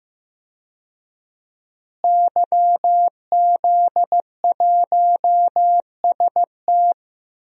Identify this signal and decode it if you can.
Morse code, heard as a keyed tone, about 15 wpm, 700 Hz. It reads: YZ1ST
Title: DXCC-100 Call Signs - 5 Characters - Repeated 3x - 15wpm